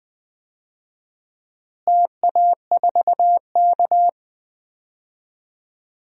Morse code, heard as a keyed tone, about 19 wpm, 700 Hz. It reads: TA4K
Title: DXCC-100 Call Signs - 4 Characters - Repeated 2x - 20wpm